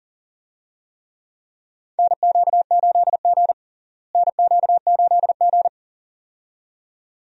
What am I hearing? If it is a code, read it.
Morse code, heard as a keyed tone, about 40 wpm, 700 Hz. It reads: NQ8G NQ8G
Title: USA Call Signs - 4 Characters - Repeated 2x - 40wpm